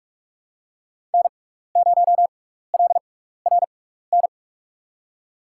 Morse code, heard as a keyed tone, about 45 wpm, spaced as for 15 wpm, 700 Hz. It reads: N0LRN
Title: USA Call Signs - 6x Character Spacing - 45wpm